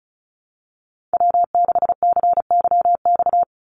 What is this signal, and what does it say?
Morse code, heard as a keyed tone, about 34 wpm, 700 Hz. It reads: W6CYX
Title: Call Signs Repeated 3x - 35wpm